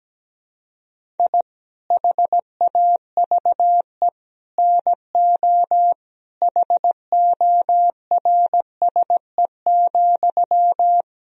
Morse code, heard as a keyed tone, about 17 wpm, 700 Hz. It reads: I HAVE NO HORSE,
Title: Sentences from Top 1000 Words - Review Part 1 - 17wpm